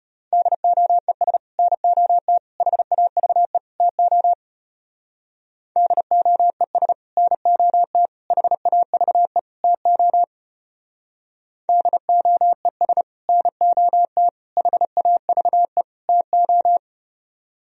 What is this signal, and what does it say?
Morse code, heard as a keyed tone, about 34 wpm, 700 Hz. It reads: DOES NOT HAVE TO DOES NOT HAVE TO DOES NOT HAVE TO